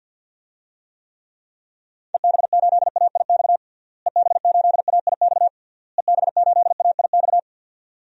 Morse code, heard as a keyed tone, about 48 wpm, 700 Hz. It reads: EB8AIX EB8AIX EB8AIX
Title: DXCC-100 Call Signs - 6 Characters - Repeated 3x - 50wpm